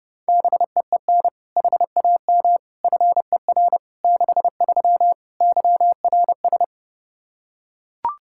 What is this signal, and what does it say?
Morse code, heard as a keyed tone, about 32 wpm, 700 Hz. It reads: BEEN HAM FER 63 YRS
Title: Practice QSO - Exchange 3 - More Info - 30wpm